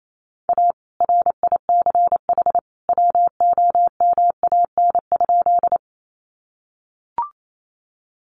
Morse code, heard as a keyed tone, about 28 wpm, 700 Hz. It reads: A RICH WOMAN?
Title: Sentences from Top 1100 Words 28wpm